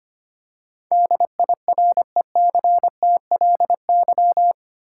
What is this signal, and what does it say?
Morse code, heard as a keyed tone, about 25 wpm, 700 Hz. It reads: DIRECTLY
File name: 8 Letter Words - First 3 Letters Given - 25wpm